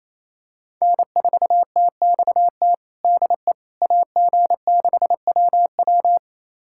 Morse code, heard as a keyed tone, about 28 wpm, 700 Hz. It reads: N4TXT DE AG6WW
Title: USA Callsign DE Callsign - 5 Characters - 28wpm